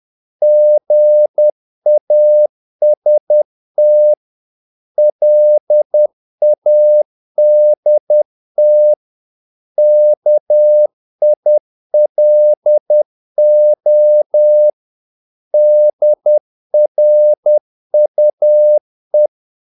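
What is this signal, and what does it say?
Morse code, heard as a keyed tone, about 10 wpm, 600 Hz. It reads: GAST LADT KILO DRUE
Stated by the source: Morse practice of short Danish words at 10 wpm, from CW med Gnister.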